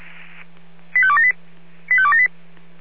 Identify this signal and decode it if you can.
Five-tone selective call (ZVEI), heard as quick runs of five tones, tones of about 70 ms. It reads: 85278 85278